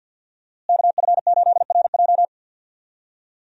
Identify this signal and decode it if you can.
Morse code, heard as a keyed tone, about 52 wpm, 700 Hz. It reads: KF8RJ
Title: USA Call Signs - 5 Characters - Repeated 3x - 50wpm